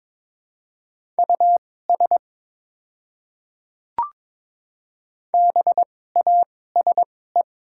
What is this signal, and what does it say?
Morse code, heard as a keyed tone, about 22 wpm, spaced as for 16 wpm, 700 Hz. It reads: US BASE
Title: 2x Character Spacing - Top 300 Words - 22wpm